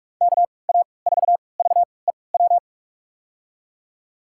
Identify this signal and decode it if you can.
Morse code, heard as a keyed tone, about 46 wpm, spaced as for 26 wpm, 700 Hz. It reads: KA4VEW
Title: USA Call Signs - 3x Character Spacing - 45wpm